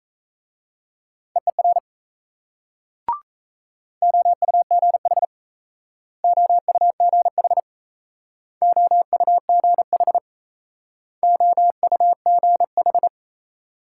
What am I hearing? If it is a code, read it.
Morse code, heard as a keyed tone, about 38 wpm, 700 Hz. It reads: EER OUGH OUGH OUGH OUGH